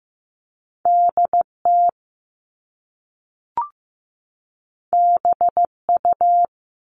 Morse code, heard as a keyed tone, about 15 wpm, 700 Hz. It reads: DT BU
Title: All Cut Numbers - 2-Digit 15wpm